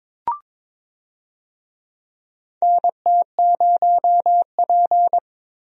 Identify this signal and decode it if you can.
Morse code, heard as a keyed tone, about 22 wpm, 700 Hz. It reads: NT0P
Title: USA Call Signs - 4 Characters - Repeated 3x - 22wpm